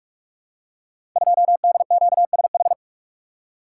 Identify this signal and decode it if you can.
Morse code, heard as a keyed tone, about 44 wpm, 700 Hz. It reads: 2DQSH
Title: Code Groups - 5 Letter:Numbers - 45wpm